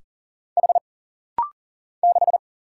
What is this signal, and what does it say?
Morse code, heard as a keyed tone, about 40 wpm, 700 Hz. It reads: H 6
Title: Single Letter-Number - Rapid-Fire 40wpm